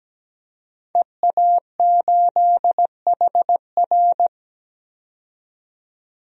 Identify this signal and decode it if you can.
Morse code, heard as a keyed tone, about 17 wpm, 700 Hz. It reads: EA8HR
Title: DXCC-100 Call Signs - 5 Characters - 17wpm